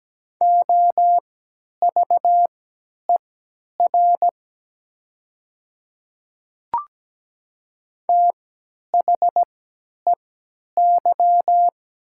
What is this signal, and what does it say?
Morse code, heard as a keyed tone, about 17 wpm, spaced as for 10 wpm, 700 Hz. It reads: OVER THEY
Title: 3x Character Spacing - Top 100 Words - 17wpm